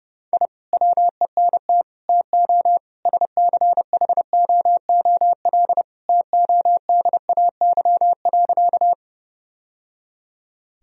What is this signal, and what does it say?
Morse code, heard as a keyed tone, about 30 wpm, 700 Hz. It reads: I WENT TO SCHOOL TODAY.
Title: Sentences from Top 1000 Words - Review Part 1 - 30wpm